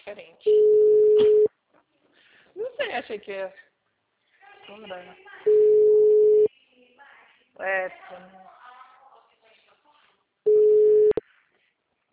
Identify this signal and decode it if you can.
Ringback tone (Europe, 425 Hz), with speech.
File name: out- -2008- - - .70